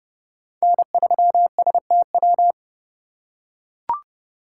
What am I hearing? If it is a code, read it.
Morse code, heard as a keyed tone, about 30 wpm, 700 Hz. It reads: N3STW
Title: USA Call Signs - 5 Characters - 4x Character Spacing - 30wpm